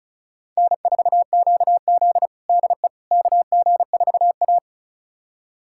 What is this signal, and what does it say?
Morse code, heard as a keyed tone, about 36 wpm, 700 Hz. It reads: N4QZ DE KG4A
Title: USA Callsign DE Callsign - 4 Characters - 35wpm